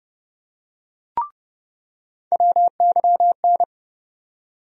Morse code, heard as a keyed tone, about 32 wpm, 700 Hz. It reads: WYN
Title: USA Call Sign Postfixes - 3 Letters 30wpm